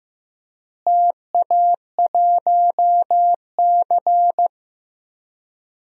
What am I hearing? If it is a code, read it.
Morse code, heard as a keyed tone, about 16 wpm, 700 Hz. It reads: TA1C
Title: DXCC-100 Call Signs - 4 Characters - Repeated 3x - 15wpm